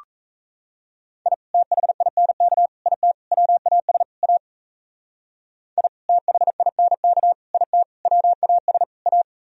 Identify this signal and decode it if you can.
Morse code, heard as a keyed tone, about 34 wpm, 700 Hz. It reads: I THINK IT WAS A I THINK IT WAS A
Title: Speed-Racing - Sets of 5 Words 28wpm